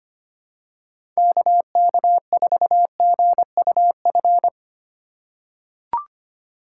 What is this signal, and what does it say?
Morse code, heard as a keyed tone, about 25 wpm, 700 Hz. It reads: KK4GUF
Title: USA Call Signs - 6 Characters - 4x Character Spacing - 25wpm